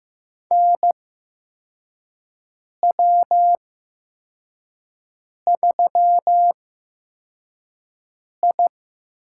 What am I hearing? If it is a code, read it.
Morse code, heard as a keyed tone, about 15 wpm, spaced as for 4 wpm, 700 Hz. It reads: NW3I
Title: USA Call Signs - 4 Characters - 8x Character Spacing - 15wpm